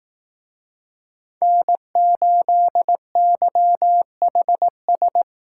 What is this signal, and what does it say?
Morse code, heard as a keyed tone, about 18 wpm, 700 Hz. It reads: N8YHS
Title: Speed Racing - USA 1x3 Call Signs 15wpm